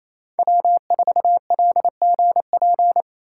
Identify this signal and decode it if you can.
Morse code, heard as a keyed tone, about 36 wpm, 700 Hz. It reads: W4LGP